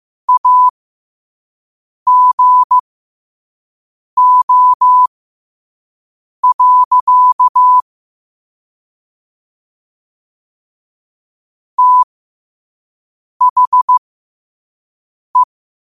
Morse code, heard as a keyed tone, about 15 wpm, spaced as for 5 wpm, 1000 Hz. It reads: AGO. THE
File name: qotdfiles / 5wpm